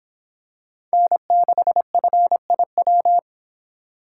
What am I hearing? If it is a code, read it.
Morse code, heard as a keyed tone, about 24 wpm, 700 Hz. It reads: N6FIW